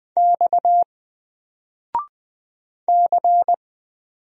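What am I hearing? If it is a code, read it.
Morse code, heard as a keyed tone, about 20 wpm, 700 Hz. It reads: X C
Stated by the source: Single letter and number Morse drill at 20 wpm, from N4LCD.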